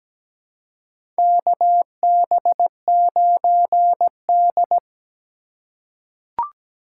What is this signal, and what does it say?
Morse code, heard as a keyed tone, about 17 wpm, 700 Hz. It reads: KB9D
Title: USA Call Signs - 4 Characters - 4x Character Spacing - 17wpm